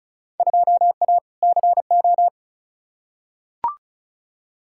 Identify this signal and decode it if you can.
Morse code, heard as a keyed tone, about 36 wpm, 700 Hz. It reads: 2A CO
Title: Field Day Exchanges - 35wpm